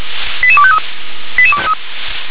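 Five-tone selective call (ZVEI), heard as quick runs of five tones, tones of about 80 ms. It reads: 88253
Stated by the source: Five-tone ZVEI selective call received with an SDR.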